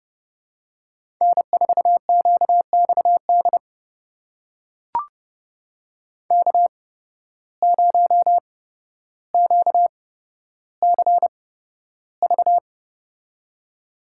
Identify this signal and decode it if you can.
Morse code, heard as a keyed tone, about 30 wpm, spaced as for 13 wpm, 700 Hz. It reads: N4QXD K0QCV
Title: USA Call Signs - 8x Character Spacing - 30wpm